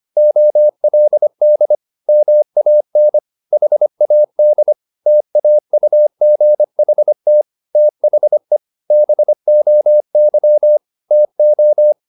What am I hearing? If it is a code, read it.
Morse code, heard as a keyed tone, about 25 wpm, 600 Hz. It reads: OLD MAN HAD TAUGHT THE BOY TO
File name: the old man and the sea part 1 25wpm